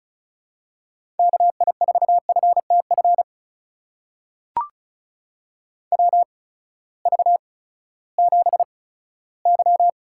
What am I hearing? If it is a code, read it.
Morse code, heard as a keyed tone, about 36 wpm, spaced as for 18 wpm, 700 Hz. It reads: KI4FTF WV7Y
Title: USA Call Signs - 8x Character Spacing - 35wpm